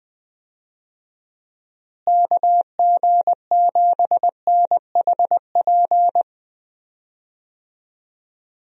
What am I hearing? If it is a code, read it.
Morse code, heard as a keyed tone, about 20 wpm, 700 Hz. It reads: KG7NHP
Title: USA Call Signs - 6 Characters - Repeated 3x - 20wpm